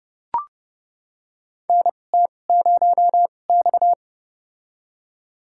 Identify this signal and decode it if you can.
Morse code, heard as a keyed tone, about 30 wpm, spaced as for 22 wpm, 700 Hz. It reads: NT0X
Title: USA Call Signs - 4 Characters - 2x Character Spacing - 30wpm